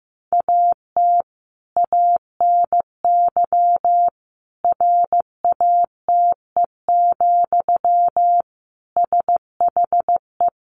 Morse code, heard as a keyed tone, about 15 wpm, 700 Hz. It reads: AT ANY RATE, SHE
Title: Sentences from Top 1300 Words 15wpm